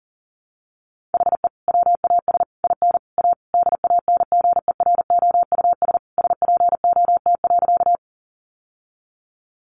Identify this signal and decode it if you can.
Morse code, heard as a keyed tone, about 37 wpm, 700 Hz. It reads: HE WAS IN A DANGEROUS SPOT.
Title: Sentences from Top 1200 Words 40wpm